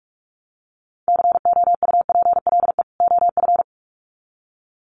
Morse code, heard as a keyed tone, about 44 wpm, 700 Hz. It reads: COUPLE OF
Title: Sets of 2 Words 45wpm